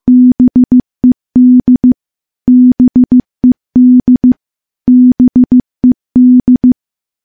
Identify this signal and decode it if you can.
Morse code, heard as a keyed tone, about 15 wpm, 250 Hz. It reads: BED BED BED
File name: morse / morse-training-15WPM